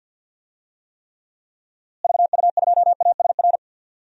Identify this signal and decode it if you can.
Morse code, heard as a keyed tone, about 52 wpm, 700 Hz. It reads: VU2ASR
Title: DXCC-100 Call Signs - 6 Characters - Repeated 2x - 50wpm